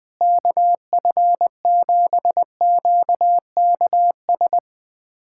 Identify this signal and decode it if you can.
Morse code, heard as a keyed tone, about 20 wpm, 700 Hz. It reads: KF7QKS
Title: USA Callsign DE Callsign - 6 Characters - 20wpm